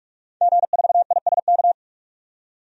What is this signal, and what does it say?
Morse code, heard as a keyed tone, about 41 wpm, 700 Hz. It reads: G4ISK
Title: DXCC-100 Call Signs - 5 Characters - 45wpm